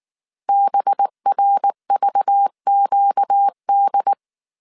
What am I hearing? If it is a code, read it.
Morse code, heard as a keyed tone, about 19 wpm, 800 Hz. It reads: BRVQD
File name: CW / DET / 1 6 1500C S A 18wpm 2255